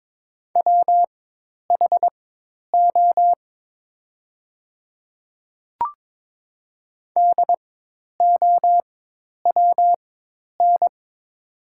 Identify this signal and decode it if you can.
Morse code, heard as a keyed tone, about 22 wpm, spaced as for 10 wpm, 700 Hz. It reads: WHO DOWN
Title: 4x Character Spacing - Top 200 Words - 22wpm